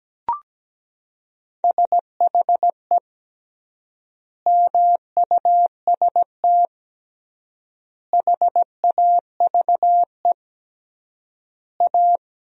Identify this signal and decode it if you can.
Morse code, heard as a keyed tone, about 17 wpm, 700 Hz. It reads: SHE MUST HAVE A
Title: 3x Word-Spacing - Sentences from Top 300 Words - 17wpm